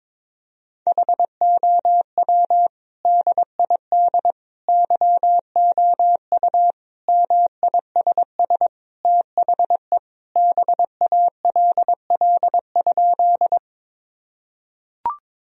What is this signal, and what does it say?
Morse code, heard as a keyed tone, about 22 wpm, 700 Hz. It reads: HOW DID YOU MISS THE BALL?
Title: Sentences from Top 1000 Words - Review Part 3 - 22wpm